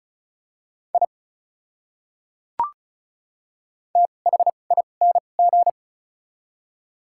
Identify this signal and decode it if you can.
Morse code, heard as a keyed tone, about 34 wpm, spaced as for 25 wpm, 700 Hz. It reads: I THING
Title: 2x Character Spacing - Top 100 Words - 35wpm